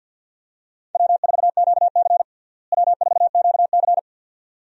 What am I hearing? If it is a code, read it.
Morse code, heard as a keyed tone, about 49 wpm, 700 Hz. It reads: W4QC W4QC
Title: USA Call Signs - 4 Characters - Repeated 2x - 50wpm